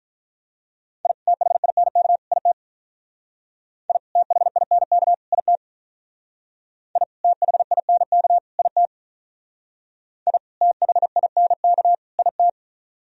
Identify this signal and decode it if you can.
Morse code, heard as a keyed tone, about 43 wpm, 700 Hz. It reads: I THINK IT I THINK IT I THINK IT I THINK IT
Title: Speed-Racing - Sets of 3 Words 35wpm